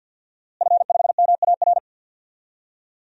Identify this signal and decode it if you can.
Morse code, heard as a keyed tone, about 52 wpm, 700 Hz. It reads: F5MAR
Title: DXCC-100 Call Signs - 5 Characters - Repeated 2x - 50wpm